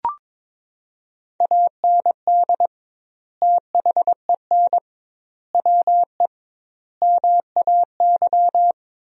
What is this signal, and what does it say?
Morse code, heard as a keyed tone, about 22 wpm, 700 Hz. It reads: AND THEN WE MAY
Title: 2x Word-Spacing - Sentences from Top 300 Words - 22wpm